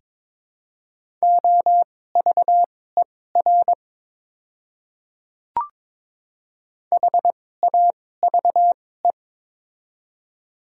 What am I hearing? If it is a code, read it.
Morse code, heard as a keyed tone, about 22 wpm, spaced as for 16 wpm, 700 Hz. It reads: OVER HAVE
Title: 2x Character Spacing - Top 200 Words - 22wpm